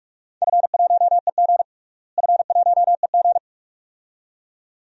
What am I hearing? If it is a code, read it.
Morse code, heard as a keyed tone, about 41 wpm, 700 Hz. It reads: F1EG F1EG
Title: DXCC-100 Call Signs - 4 Characters - Repeated 2x - 45wpm